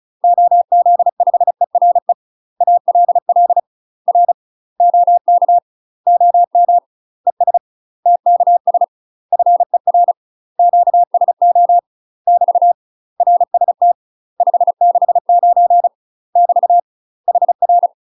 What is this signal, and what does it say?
Morse code, heard as a keyed tone, about 34 wpm, 700 Hz. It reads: OZ5ERE ALL R OK OM ES TKS FER QSO <BT> RST 569 <BT> HR
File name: UK MASTER QSOs TRAINING / 35WPM / UK MASTER QSOs 0005